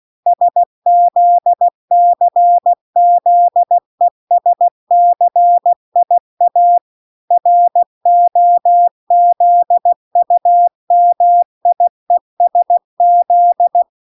Morse code, heard as a keyed tone, about 16 wpm, 700 Hz. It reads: SZCZESCIA ROZUMIESZ